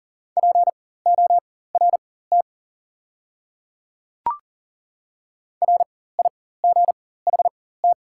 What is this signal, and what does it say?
Morse code, heard as a keyed tone, about 40 wpm, spaced as for 19 wpm, 690 Hz. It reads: PORT RIGHT
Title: 4x Character Spacing - Top 300 Words - 40wpm